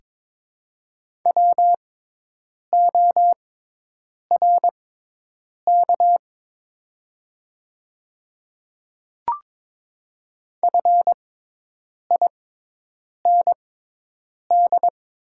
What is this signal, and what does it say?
Morse code, heard as a keyed tone, about 22 wpm, spaced as for 8 wpm, 700 Hz. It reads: WORK FIND
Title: 6x Character Spacing - Top 200 Words - 22wpm